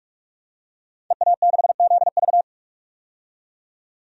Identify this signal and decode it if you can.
Morse code, heard as a keyed tone, about 42 wpm, 700 Hz. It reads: EA6ZV
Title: DXCC-100 Call Signs - 5 Characters - 45wpm